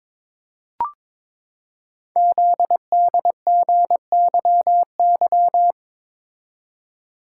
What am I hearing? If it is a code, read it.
Morse code, heard as a keyed tone, about 22 wpm, 700 Hz. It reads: ZDGYY
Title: Code Groups - 5 Letters - 22wpm